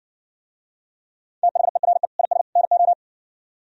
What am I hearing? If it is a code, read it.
Morse code, heard as a keyed tone, about 60 wpm, 700 Hz. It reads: THERE IS NO